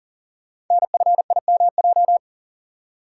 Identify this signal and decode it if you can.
Morse code, heard as a keyed tone, about 42 wpm, 690 Hz. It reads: NFIMJ
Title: Code Groups - 5 Letters - 40wpm